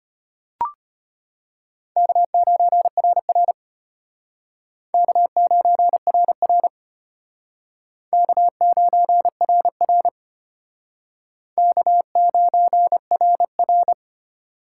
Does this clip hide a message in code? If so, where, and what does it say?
Yes: Morse code from 2.0 to 13.9 s, heard as a keyed tone, about 32 wpm, 700 Hz. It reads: K9RR K9RR K9RR K9RR